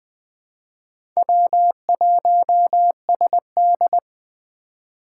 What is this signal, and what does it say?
Morse code, heard as a keyed tone, about 20 wpm, 700 Hz. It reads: W1SD
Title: USA Call Signs - 4 Characters - 4x Character Spacing - 20wpm